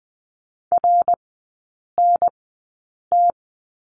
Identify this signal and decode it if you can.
Morse code, heard as a keyed tone, about 20 wpm, spaced as for 8 wpm, 700 Hz. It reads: RNT
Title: 100 Lesson 17e - Letter u - Three Characters 20wpm